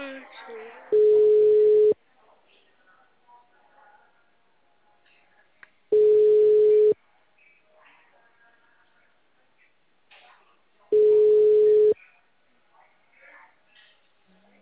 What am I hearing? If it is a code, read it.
Ringback tone (Europe, 425 Hz), with speech.